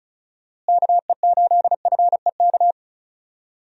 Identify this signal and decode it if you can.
Morse code, heard as a keyed tone, about 34 wpm, 700 Hz. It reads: KE8FEK
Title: USA Call Signs - 6 Characters - Repeated 3x - 35wpm